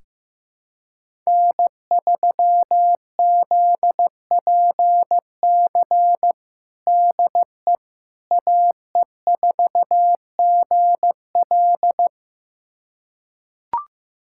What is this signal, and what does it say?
Morse code, heard as a keyed tone, about 15 wpm, 700 Hz. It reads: N3ZPC DE AE4GL
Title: USA Callsign DE Callsign - 5 Characters - 15wpm